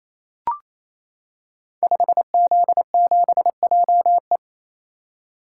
Morse code, heard as a keyed tone, about 29 wpm, 700 Hz. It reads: 5Z7JE
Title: Code Groups - 5 Letter:Numbers - 28wpm